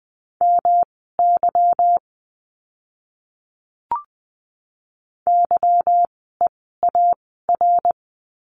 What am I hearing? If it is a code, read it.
Morse code, heard as a keyed tone, about 20 wpm, spaced as for 15 wpm, 700 Hz. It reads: MY YEAR
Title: 185 Lesson 30f - Letter b - Words 20wpm